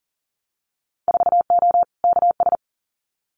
Morse code, heard as a keyed tone, about 40 wpm, 700 Hz. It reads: 4O KS
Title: Winter Field Day Exchanges Encore 40wpm